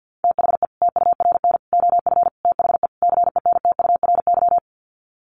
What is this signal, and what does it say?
Morse code, heard as a keyed tone, about 50 wpm, 700 Hz. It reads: THE TURN OF THE CENTURY
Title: Sets of 5 Words 50wpm